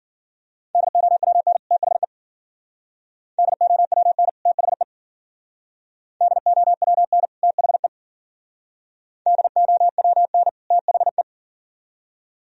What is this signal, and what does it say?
Morse code, heard as a keyed tone, about 50 wpm, 700 Hz. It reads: DOWN THE DOWN THE DOWN THE DOWN THE
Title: Speed-Racing - Sets of 2 Words 40wpm